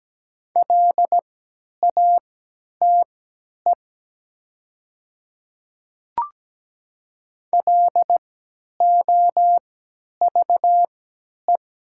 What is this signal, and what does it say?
Morse code, heard as a keyed tone, about 17 wpm, spaced as for 10 wpm, 700 Hz. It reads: LATE LOVE
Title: 3x Character Spacing - Top 500 Words - 17wpm